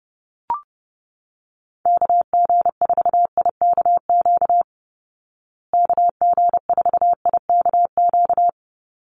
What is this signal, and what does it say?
Morse code, heard as a keyed tone, about 30 wpm, 700 Hz. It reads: KG4IKQ KG4IKQ
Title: Call Signs Repeated 2x - 30wpm